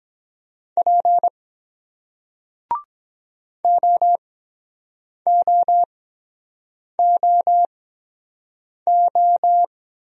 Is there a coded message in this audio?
Yes: Morse code from 0.8 to 9.6 s, heard as a keyed tone, about 24 wpm, 700 Hz. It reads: P O O O O